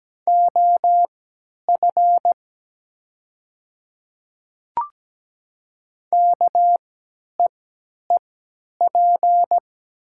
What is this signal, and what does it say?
Morse code, heard as a keyed tone, about 17 wpm, spaced as for 10 wpm, 700 Hz. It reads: OF KEEP
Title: 3x Character Spacing - Top 300 Words - 17wpm